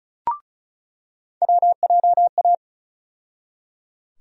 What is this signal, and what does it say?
Morse code, heard as a keyed tone, about 36 wpm, 700 Hz. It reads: WJA
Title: USA Call Sign Postfixes - 3 Letters 35wpm